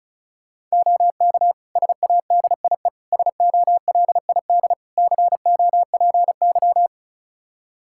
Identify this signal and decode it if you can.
Morse code, heard as a keyed tone, about 36 wpm, 700 Hz. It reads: OK SADIE SOLID COPY
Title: Practice QSO - Exchange 3 - More Info - 35wpm